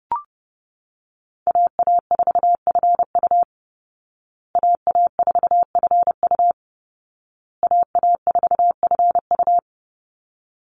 Morse code, heard as a keyed tone, about 30 wpm, 700 Hz. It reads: AA4FU AA4FU AA4FU
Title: Call Signs Repeated 2x - 30wpm